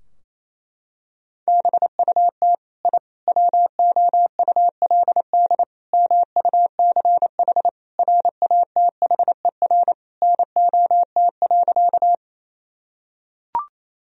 Morse code, heard as a keyed tone, about 28 wpm, 700 Hz. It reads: BUT I WOULD MUCH RATHER NOT.
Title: Sentences from Top 1000 Words - Review Part 4 - 28wpm